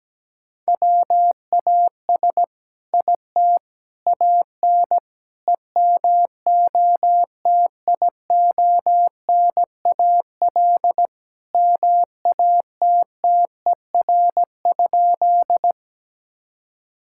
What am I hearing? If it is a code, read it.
Morse code, heard as a keyed tone, about 17 wpm, 700 Hz. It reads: WAS IT AN EMOTIONAL MATTER?
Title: Sentences from Top 1000 Words - Review Part 5 - 17wpm